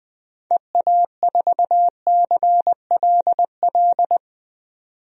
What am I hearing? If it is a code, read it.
Morse code, heard as a keyed tone, about 20 wpm, 700 Hz. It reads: EA4CLL
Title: DXCC-100 Call Signs - 6 Characters - 20wpm